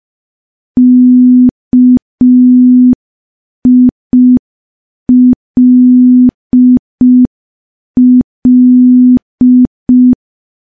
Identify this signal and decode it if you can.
Morse code, heard as a keyed tone, about 5 wpm, 250 Hz. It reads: KILL
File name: morse / morse-training-05WPM